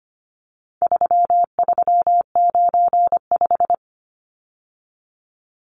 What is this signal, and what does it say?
Morse code, heard as a keyed tone, about 25 wpm, 700 Hz. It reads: 3395
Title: Numbers - 4-Digit 25wpm